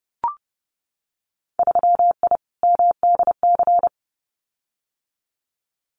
Morse code, heard as a keyed tone, about 30 wpm, 700 Hz. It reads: 3I MDC
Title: Winter Field Day Exchanges Encore 30wpm